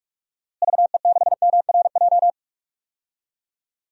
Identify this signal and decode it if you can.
Morse code, heard as a keyed tone, about 41 wpm, 700 Hz. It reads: VE6MRJ
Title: DXCC-100 Call Signs - 6 Characters - Repeated 2x - 45wpm